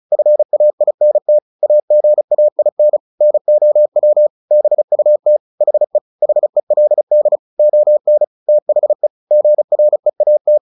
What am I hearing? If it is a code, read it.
Morse code, heard as a keyed tone, about 34 wpm, 600 Hz. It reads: FAINT AGAIN NOW BUT HE HELD ON THE GREAT
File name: the old man and the sea 2 35wpm